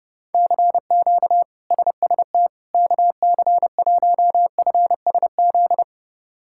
Morse code, heard as a keyed tone, about 30 wpm, 700 Hz. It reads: CQ SST KC1FSZ
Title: K1USN SST QSO Practice - 30wpm